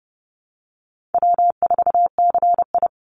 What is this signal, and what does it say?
Morse code, heard as a keyed tone, about 30 wpm, 700 Hz. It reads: W4CI
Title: Easy Call Signs Repeated 2x - 30wpm